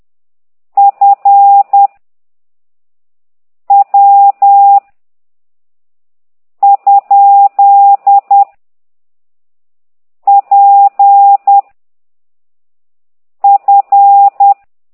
Morse code, heard as a keyed tone, about 10 wpm, 800 Hz. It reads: F W ? P F
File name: ct2idw / exerc15